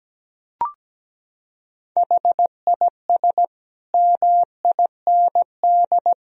Morse code, heard as a keyed tone, about 17 wpm, 700 Hz. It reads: HIS MIND
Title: Sentences from Top 1000 Words - Review Part 4 - 17wpm